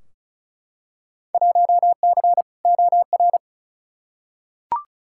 Morse code, heard as a keyed tone, about 36 wpm, 700 Hz. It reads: 1C OR
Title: Field Day Exchanges - 35wpm